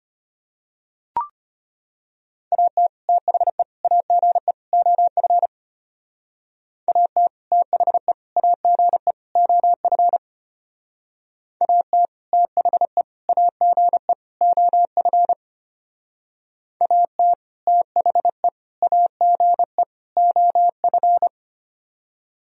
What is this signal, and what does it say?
Morse code, heard as a keyed tone, about 33 wpm, 700 Hz. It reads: AT THE AGE OF AT THE AGE OF AT THE AGE OF AT THE AGE OF